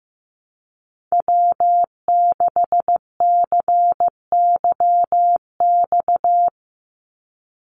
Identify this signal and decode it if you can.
Morse code, heard as a keyed tone, about 15 wpm, 700 Hz. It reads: W6CYX
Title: Call Signs Repeated 3x - 15wpm